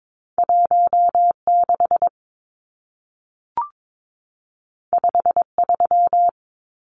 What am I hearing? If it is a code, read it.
Morse code, heard as a keyed tone, about 22 wpm, 700 Hz. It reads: 16 53
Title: Numbers - 2-Digit 22wpm